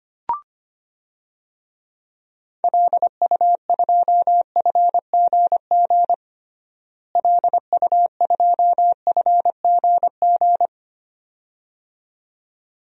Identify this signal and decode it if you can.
Morse code, heard as a keyed tone, about 25 wpm, 700 Hz. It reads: LU2FGG LU2FGG
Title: DXCC-100 Call Signs - 6 Characters - Repeated 2x - 25wpm